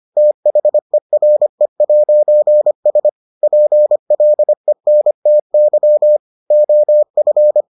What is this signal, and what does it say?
Morse code, heard as a keyed tone, about 25 wpm, 600 Hz. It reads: THERE'S PLENTY OF